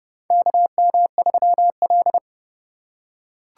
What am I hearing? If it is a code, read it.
Morse code, heard as a keyed tone, about 30 wpm, 700 Hz. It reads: KM3L
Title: USA Call Signs - 4 Characters - 8x Character Spacing - 30wpm